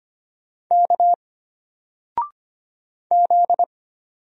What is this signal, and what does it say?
Morse code, heard as a keyed tone, about 25 wpm, 700 Hz. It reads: K Z
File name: Single-Letter-Number-25wpm